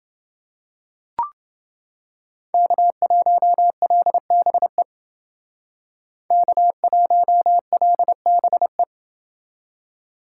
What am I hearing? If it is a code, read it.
Morse code, heard as a keyed tone, about 28 wpm, 700 Hz. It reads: K1LBE K1LBE